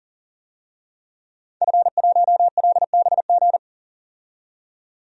Morse code, heard as a keyed tone, about 40 wpm, 700 Hz. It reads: F1LBG
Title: DXCC-100 Call Signs - 5 Characters - 40wpm